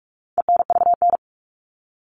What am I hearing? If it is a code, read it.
Morse code, heard as a keyed tone, about 44 wpm, 700 Hz. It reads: ENVN
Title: All Cut Numbers - 4-Digit 45wpm